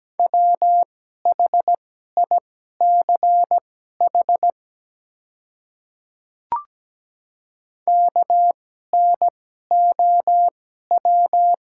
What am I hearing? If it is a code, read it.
Morse code, heard as a keyed tone, about 17 wpm, spaced as for 12 wpm, 700 Hz. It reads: WHICH KNOW
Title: 2x Character Spacing - Top 200 Words - 17wpm